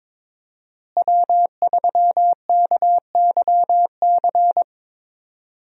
Morse code, heard as a keyed tone, about 22 wpm, 700 Hz. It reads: W3KYC
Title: USA Call Signs - 5 Characters - Repeated 2x - 22wpm